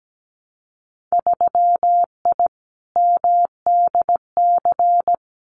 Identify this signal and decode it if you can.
Morse code, heard as a keyed tone, about 17 wpm, 700 Hz. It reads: 3I MDC
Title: Winter Field Day Exchanges Encore 17wpm